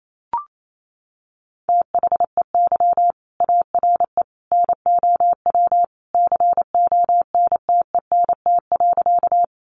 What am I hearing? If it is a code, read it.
Morse code, heard as a keyed tone, about 28 wpm, 700 Hz. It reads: THEY ARE NOW CONTENT.
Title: Sentences from Top 900 Words 28wpm